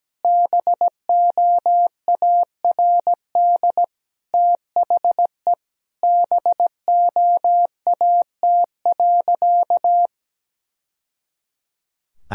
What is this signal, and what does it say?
Morse code, heard as a keyed tone, about 17 wpm, 700 Hz. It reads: BOARD THE BOAT.
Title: Sentences from Top 1000 Words - Review Part 4 - 17wpm